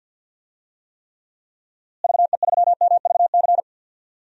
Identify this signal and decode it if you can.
Morse code, heard as a keyed tone, about 52 wpm, 700 Hz. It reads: VE3MVC
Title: DXCC-100 Call Signs - 6 Characters - Repeated 3x - 50wpm